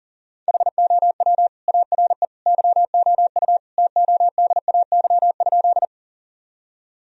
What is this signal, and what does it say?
Morse code, heard as a keyed tone, about 37 wpm, 700 Hz. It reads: HOW ARE YOU TODAY?
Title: Sentences from Top 1000 Words - Review Part 4 - 40wpm